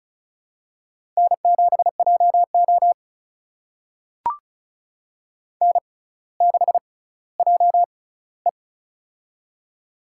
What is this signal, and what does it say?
Morse code, heard as a keyed tone, about 36 wpm, spaced as for 18 wpm, 700 Hz. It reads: N7JO N6JE
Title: USA Call Signs - 4 Characters - 6x Character Spacing - 35wpm